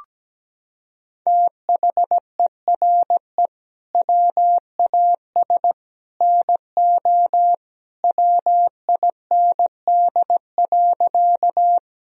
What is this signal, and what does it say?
Morse code, heard as a keyed tone, about 17 wpm, 700 Hz. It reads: THERE WAS NO WIND.
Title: Sentences from Top 1000 Words - Review Part 2 - 17wpm